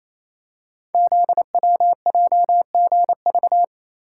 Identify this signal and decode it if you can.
Morse code, heard as a keyed tone, about 27 wpm, 700 Hz. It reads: ZWJGV
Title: Code Groups - 5 Letters - 28wpm